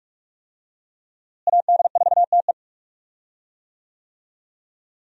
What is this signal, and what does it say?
Morse code, heard as a keyed tone, about 46 wpm, 700 Hz. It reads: AD4TE
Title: USA Call Signs - 5 Characters - Repeated 2x - 45wpm